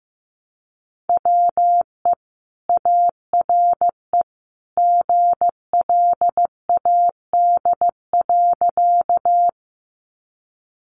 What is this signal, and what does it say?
Morse code, heard as a keyed tone, about 15 wpm, 700 Hz. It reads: WE ARE GLAD.
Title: Sentences from Top 800 Words 15wpm